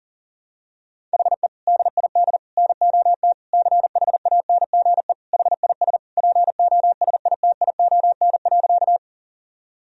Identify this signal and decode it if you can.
Morse code, heard as a keyed tone, about 40 wpm, 700 Hz. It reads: HE DID NOT CHANGE HIS POSITION.
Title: Sentences from Top 1000 Words - Review Part 1 - 40wpm